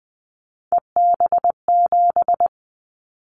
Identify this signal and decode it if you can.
Morse code, heard as a keyed tone, about 20 wpm, 700 Hz. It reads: EB7
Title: Callsign Prefixes for Top 25 DXCC - 20wpm